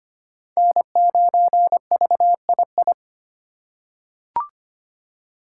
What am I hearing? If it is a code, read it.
Morse code, heard as a keyed tone, about 25 wpm, 700 Hz. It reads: N9VII
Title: USA Call Signs - 5 Characters - 3x Character Spacing - 25wpm